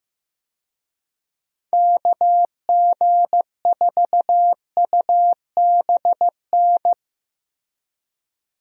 Morse code, heard as a keyed tone, about 15 wpm, 700 Hz. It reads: KG4UBN
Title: USA Call Signs - 6 Characters - Repeated 3x - 15wpm